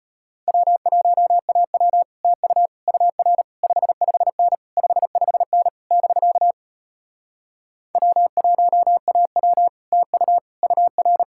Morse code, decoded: W1AW TU UR 55N 55N <BK> W1AW TU UR

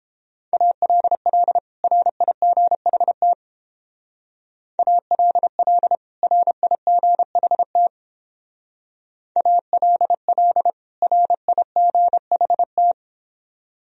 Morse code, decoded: ALL RIGHT ALL RIGHT ALL RIGHT

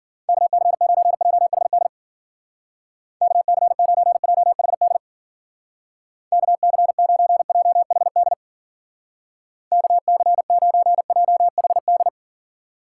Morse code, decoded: KC9JHD KC9JHD KC9JHD KC9JHD

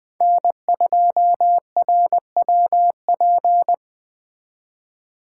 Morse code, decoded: N2RWP

